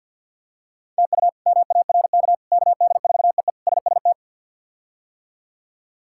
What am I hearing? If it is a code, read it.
Morse code, heard as a keyed tone, about 46 wpm, 700 Hz. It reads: TU MARK KD4EE SST
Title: K1USN SST QSO Practice - 50wpm